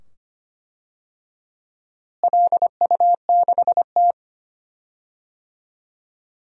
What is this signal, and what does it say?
Morse code, heard as a keyed tone, about 25 wpm, 700 Hz. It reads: LU6T
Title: DXCC-100 Call Signs - 4 Characters - Repeated 2x - 25wpm